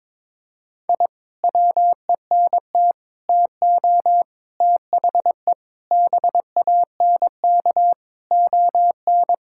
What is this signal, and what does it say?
Morse code, heard as a keyed tone, about 22 wpm, 700 Hz. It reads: I WENT TO THE BANK ON
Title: Sentences from Top 1000 Words - Review Part 6 - 22wpm